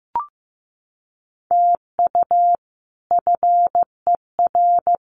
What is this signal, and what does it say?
Morse code, heard as a keyed tone, about 15 wpm, 700 Hz. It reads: TU FER
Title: Common QSO Phrases Encore 15wpm